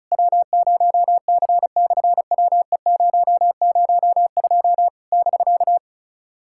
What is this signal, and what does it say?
Morse code, heard as a keyed tone, about 36 wpm, 700 Hz. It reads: W0C/WE002 <BK>
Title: SOTA QSO Practice 35wpm